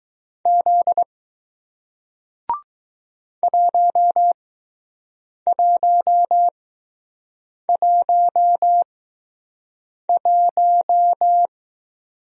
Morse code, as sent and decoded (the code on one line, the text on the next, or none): --.. / .---- .---- .---- .----
Z 1111